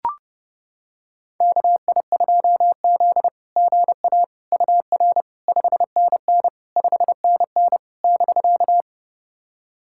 Morse code, heard as a keyed tone, about 32 wpm, 690 Hz. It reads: KI2Z GA UR 5NN 5NN <BK>
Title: SOTA QSO Practice 30wpm